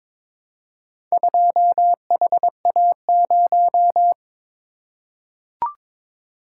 Morse code, heard as a keyed tone, about 22 wpm, 700 Hz. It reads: 2HA0
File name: practice / 22wpm / Four Letter-Number Combinations - 22wpm